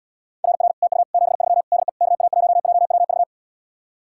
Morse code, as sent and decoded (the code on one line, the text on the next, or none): .-. .-. / - ..- / --... ...-- / -.. / -.- .- .---- .--- .-- ...-
RR TU 73 DE KA1JWV